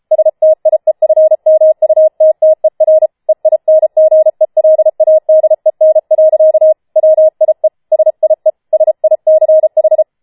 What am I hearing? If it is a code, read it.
Morse code, heard as a keyed tone, about 33 wpm, 630 Hz. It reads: STIEFMUTTER EINGELADEN. WIE SIE SICH